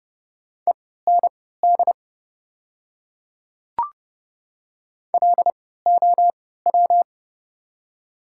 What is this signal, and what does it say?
Morse code, heard as a keyed tone, about 30 wpm, spaced as for 17 wpm, 690 Hz. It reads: END LOW
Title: 3x Character Spacing - Top 200 Words - 30wpm